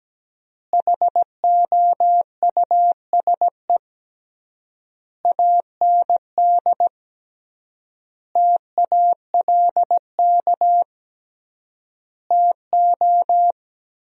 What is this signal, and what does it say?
Morse code, heard as a keyed tone, about 17 wpm, 700 Hz. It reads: HOUSE AND TALK TO
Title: 3x Word-Spacing - Sentences from Top 400 Words - 17wpm